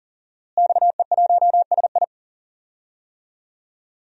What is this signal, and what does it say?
Morse code, heard as a keyed tone, about 40 wpm, 700 Hz. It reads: XE1SI
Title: DXCC-100 Call Signs - 5 Characters - Repeated 3x - 40wpm